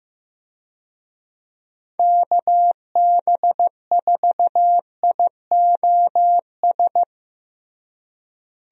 Morse code, heard as a keyed tone, about 15 wpm, 700 Hz. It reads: KB4IOS
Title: USA Call Signs - 6 Characters - Repeated 2x - 15wpm